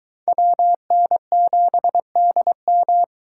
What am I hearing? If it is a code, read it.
Morse code, heard as a keyed tone, about 21 wpm, 700 Hz. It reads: WN7DM